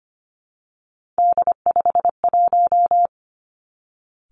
Morse code, heard as a keyed tone, about 25 wpm, 700 Hz. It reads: D51
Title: Three Challenging Letter-Number Combinations 25wpm